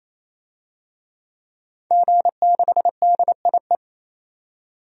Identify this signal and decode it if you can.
Morse code, heard as a keyed tone, about 28 wpm, 700 Hz. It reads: G6DIE
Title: DXCC-100 Call Signs - 5 Characters - 28wpm